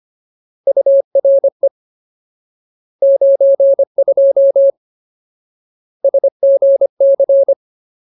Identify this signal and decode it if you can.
Morse code, heard as a keyed tone, about 25 wpm, 550 Hz. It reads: URE 92 SGC
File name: Ultimate Word List / 25WPM Ultimate